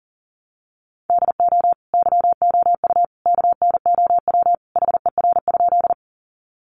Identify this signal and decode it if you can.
Morse code, heard as a keyed tone, about 38 wpm, 700 Hz. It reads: DO YOU KNOW HER?
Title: Sentences from Top 100 Words 40wpm